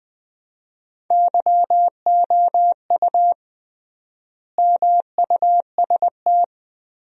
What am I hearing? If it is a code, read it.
Morse code, heard as a keyed tone, about 20 wpm, 700 Hz. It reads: YOU MUST
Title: 3x Word-Spacing - Sentences from Top 300 Words - 20wpm